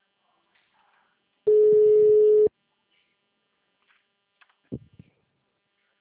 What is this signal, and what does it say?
Ringback tone (Europe, 425 Hz).
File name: out- -2002- - - .1492